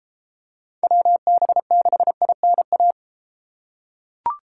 Morse code, heard as a keyed tone, about 33 wpm, 700 Hz. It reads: WB6INA